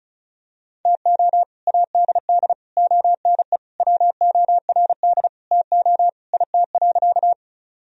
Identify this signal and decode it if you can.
Morse code, heard as a keyed tone, about 36 wpm, 700 Hz. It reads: TO ADD ONE WORD TO IT.
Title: Sentences from Top 1000 Words - Review Part 1 - 35wpm